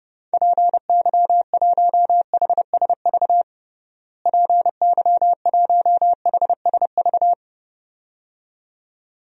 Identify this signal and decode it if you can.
Morse code, heard as a keyed tone, about 30 wpm, 700 Hz. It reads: PY1HSV PY1HSV
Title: DXCC-100 Call Signs - 6 Characters - Repeated 2x - 30wpm